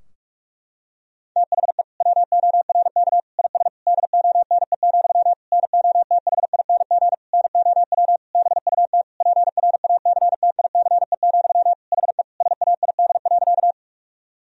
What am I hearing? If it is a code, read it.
Morse code, heard as a keyed tone, about 46 wpm, 700 Hz. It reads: THE WORK IS DONE, NOTHING NOW BUT PRACTICE, HE SAID.
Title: Sentences from Top 1000 Words - Review Part 4 - 45wpm